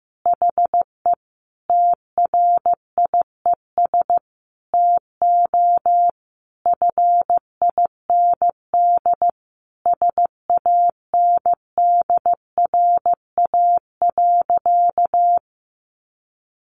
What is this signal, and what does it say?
Morse code, heard as a keyed tone, about 14 wpm, 700 Hz. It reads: HE TRIES TO FIND SANDRA.
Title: Easy Phrases 15wpm